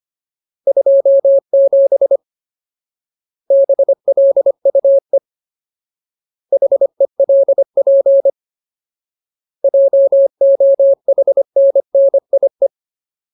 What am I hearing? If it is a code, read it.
Morse code, heard as a keyed tone, about 25 wpm, 550 Hz. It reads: 27 BLUE HELP JOHNNIE
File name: Ultimate Word List / 25WPM Ultimate